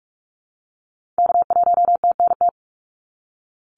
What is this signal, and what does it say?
Morse code, heard as a keyed tone, about 46 wpm, 700 Hz. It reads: K1TNT
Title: NA SOTA Activator Callsigns 45wpm